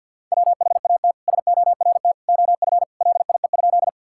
Morse code, heard as a keyed tone, about 50 wpm, 700 Hz. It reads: WHAT SORT OF LIE?
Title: Sentences from Top 1600 Words 50wpm